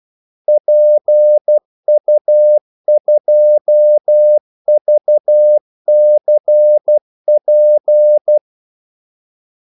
Morse code, heard as a keyed tone, about 12 wpm, 600 Hz. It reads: PU2VCP
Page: 2 ekstra ordmellemrum mellem kaldesignalerne